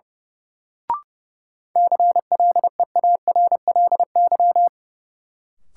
Morse code, heard as a keyed tone, about 30 wpm, 700 Hz. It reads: CLEARLY
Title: Sentences from Top 1300 Words - New Word Drill Alt - 30wpm